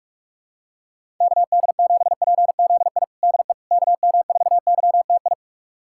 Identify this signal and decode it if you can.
Morse code, heard as a keyed tone, about 42 wpm, 700 Hz. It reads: KD7PZI DE KM4YTI
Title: USA Callsign DE Callsign - 6 Characters - 45wpm